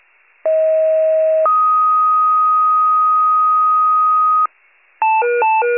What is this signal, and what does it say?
Two-tone page, heard as a one-second tone, then a three-second tone, about 640 then 1250 Hz. It is heard from 0.5 to 4.5 s.
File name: wcalert